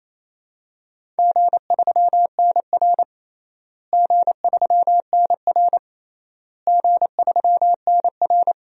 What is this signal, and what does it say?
Morse code, heard as a keyed tone, about 28 wpm, 700 Hz. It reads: G3NR G3NR G3NR
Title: DXCC-100 Call Signs - 4 Characters - Repeated 3x - 28wpm